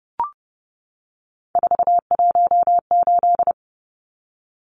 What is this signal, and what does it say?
Morse code, heard as a keyed tone, about 29 wpm, 700 Hz. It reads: 418
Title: Numbers - 3-Digit 30wpm